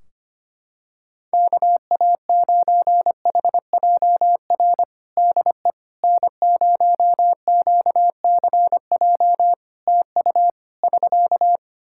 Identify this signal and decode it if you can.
Morse code, heard as a keyed tone, about 25 wpm, 700 Hz. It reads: KA9HJR DE N0QCJ TU <SK>
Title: Practice QSO - Exchange 4 - Ending - 25wpm